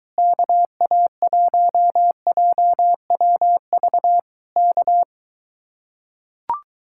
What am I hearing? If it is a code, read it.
Morse code, heard as a keyed tone, about 23 wpm, 700 Hz. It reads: KA1JWV K